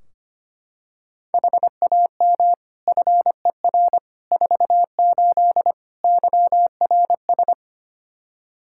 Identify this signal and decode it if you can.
Morse code, heard as a keyed tone, about 25 wpm, 700 Hz. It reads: HAM FER 48 YRS
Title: K1USN SST QSO Practice 25wpm